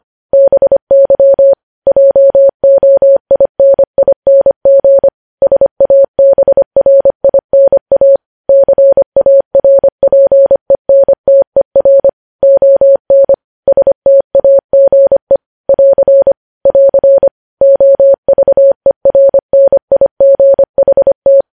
Morse code, heard as a keyed tone, about 25 wpm, 550 Hz. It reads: BY JOINING SABRINA CARPENTER ON STAGE <AR> <AR> OVERNIGHT
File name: podcast / News headlines-1x-25x25